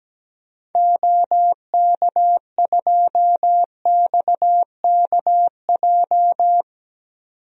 Morse code, heard as a keyed tone, about 17 wpm, 700 Hz. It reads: OK2XKJ
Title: DXCC-100 Call Signs - 6 Characters - 17wpm